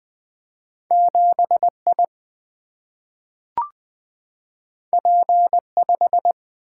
Morse code, heard as a keyed tone, about 20 wpm, 700 Hz. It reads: 7 I P 5
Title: Single-Letter Single-Number 20wpm